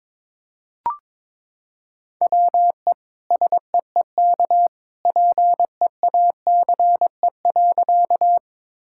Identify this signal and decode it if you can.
Morse code, heard as a keyed tone, about 22 wpm, 700 Hz. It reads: WE SEEK PEACE.
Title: Sentences from Top 1600 Words 22wpm